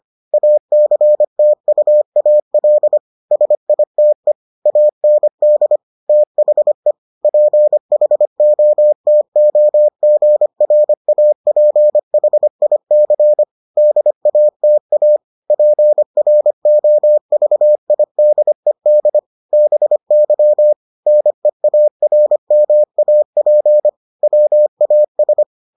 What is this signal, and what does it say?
Morse code, heard as a keyed tone, about 25 wpm, 600 Hz. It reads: ACTUAL SITE AND THE PHOTOGRAPHIC DATA PROVIDED BY NEARMAP WAS